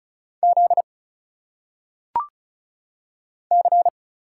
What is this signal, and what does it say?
Morse code, heard as a keyed tone, about 34 wpm, 700 Hz. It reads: Z C
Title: Single Letters - Phonetic Pronunciation 35wpm